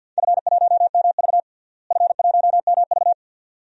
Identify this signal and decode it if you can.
Morse code, heard as a keyed tone, about 50 wpm, 700 Hz. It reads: F1MV F1MV
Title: DXCC-100 Call Signs - 4 Characters - Repeated 2x - 50wpm